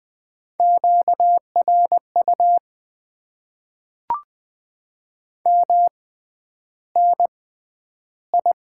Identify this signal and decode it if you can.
Morse code, heard as a keyed tone, about 20 wpm, spaced as for 10 wpm, 700 Hz. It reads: QRU MNI
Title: Top 90 QSO Elements - 6x Character Spacing - 20wpm